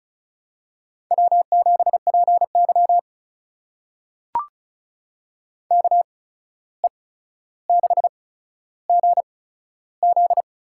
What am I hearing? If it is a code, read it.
Morse code, heard as a keyed tone, about 36 wpm, spaced as for 14 wpm, 700 Hz. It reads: W7PY KE6GZ
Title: USA Call Signs - 8x Character Spacing - 35wpm